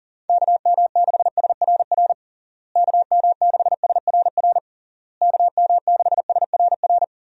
Morse code, heard as a keyed tone, about 38 wpm, 700 Hz. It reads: KM6SRR KM6SRR KM6SRR
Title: USA Call Signs - 6 Characters - Repeated 3x - 40wpm